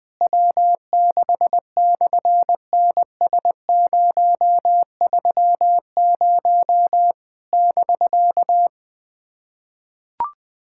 Morse code, heard as a keyed tone, about 18 wpm, 700 Hz. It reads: W6/NS030 <BK>
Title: SOTA QSO Practice 20wpm